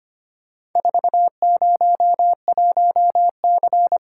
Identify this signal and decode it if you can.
Morse code, heard as a keyed tone, about 25 wpm, 700 Hz. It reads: 401C
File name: practice / 25wpm / Four Letter-Number Combinations - 25wpm